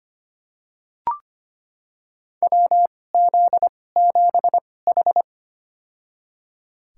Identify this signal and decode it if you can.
Morse code, heard as a keyed tone, about 25 wpm, spaced as for 18 wpm, 700 Hz. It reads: WZ7H
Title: USA Call Signs - 2x Character Spacing - 25wpm